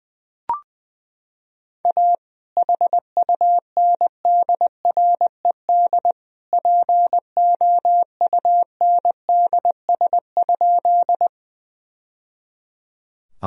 Morse code, heard as a keyed tone, about 20 wpm, 700 Hz. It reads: A HUNDRED POUNDS?
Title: Sentences from Top 1000 Words - Review Part 2 - 20wpm